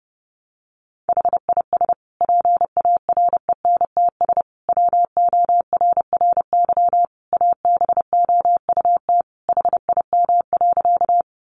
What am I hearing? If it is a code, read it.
Morse code, heard as a keyed tone, about 30 wpm, 700 Hz. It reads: HIS PARENTS WORRY ABOUT HIM.
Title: Easy Phrases 30wpm